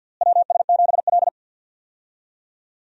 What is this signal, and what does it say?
Morse code, heard as a keyed tone, about 50 wpm, 700 Hz. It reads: WS7L
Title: USA Call Signs - 4 Characters - Repeated 2x - 50wpm